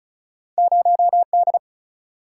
Morse code, heard as a keyed tone, about 36 wpm, 700 Hz. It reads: 0D
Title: Single-Letter Single-Number 35wpm